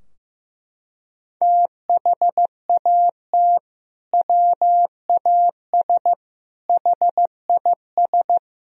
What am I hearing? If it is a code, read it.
Morse code, heard as a keyed tone, about 15 wpm, 700 Hz. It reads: THAT WAS HIS
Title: Sentences from Top 1000 Words - Review Part 2 - 15wpm